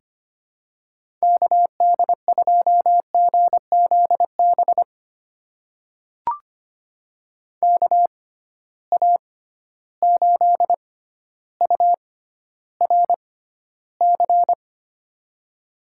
Morse code, heard as a keyed tone, about 25 wpm, spaced as for 13 wpm, 700 Hz. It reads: KD2GZB KA8URC
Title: USA Call Signs - 6 Characters - 6x Character Spacing - 25wpm